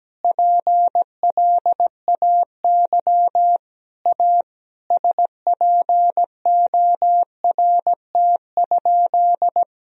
Morse code, heard as a keyed tone, about 17 wpm, 700 Hz. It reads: PLAY A SPORT?
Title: Sentences from Top 1000 Words - Review Part 4 - 17wpm